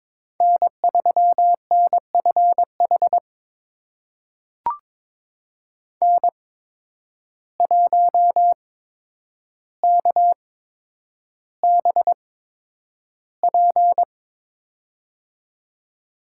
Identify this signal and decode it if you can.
Morse code, heard as a keyed tone, about 22 wpm, spaced as for 9 wpm, 700 Hz. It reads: N3NFH N1KBP
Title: USA Call Signs - 5 Characters - 8x Character Spacing - 22wpm